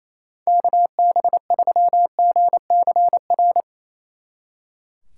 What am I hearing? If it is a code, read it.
Morse code, heard as a keyed tone, about 34 wpm, 700 Hz. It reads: KB3GCR
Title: Speed Racing - USA 2x3 Call Signs 28wpm